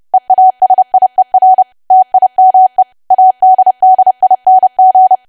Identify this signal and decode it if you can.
Morse code, heard as a keyed tone, about 30 wpm, 750 Hz. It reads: EASIER TIME ADDING
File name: cwa / W1AW-Level3 / 30WPM